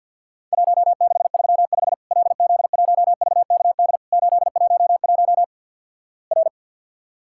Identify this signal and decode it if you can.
Morse code, heard as a keyed tone, about 50 wpm, 700 Hz. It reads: 1635 LZ1VKD 811 R
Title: Worked All Europe DX Contest - QTC Exchange - 50wpm